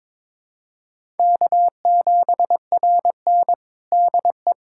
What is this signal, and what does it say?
Morse code, heard as a keyed tone, about 22 wpm, 700 Hz. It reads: K7RN DE
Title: USA Callsign DE Callsign - 4 Characters - 22wpm